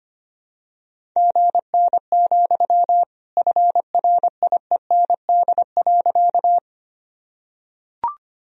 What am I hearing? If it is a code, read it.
Morse code, heard as a keyed tone, about 25 wpm, 700 Hz. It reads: GN, FRIEND.
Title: Sentences from Top 300 Words and Abbreviations 25wpm